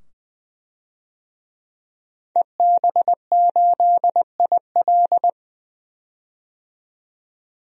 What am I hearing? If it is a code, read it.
Morse code, heard as a keyed tone, about 21 wpm, 700 Hz. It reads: EB8IL
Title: DXCC-100 Call Signs - 5 Characters - 20wpm